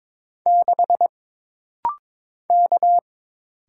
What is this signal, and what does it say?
Morse code, heard as a keyed tone, about 22 wpm, 700 Hz. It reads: 6 K
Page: This rapid-fire timing is geared to push your ICR (Instant Character Recognition) skills.